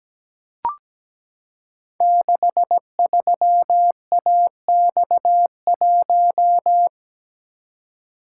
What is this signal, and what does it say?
Morse code, heard as a keyed tone, about 17 wpm, 700 Hz. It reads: 63AX1
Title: Code Groups - 5 Letter:Numbers - 17wpm